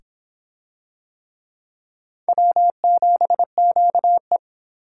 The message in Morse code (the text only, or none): W7QE